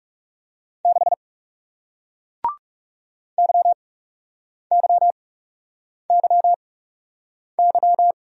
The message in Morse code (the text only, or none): 6 YYYY